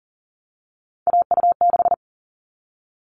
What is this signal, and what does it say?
Morse code, heard as a keyed tone, about 40 wpm, 700 Hz. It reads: AU6
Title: All Cut Numbers - 3-Digit 40wpm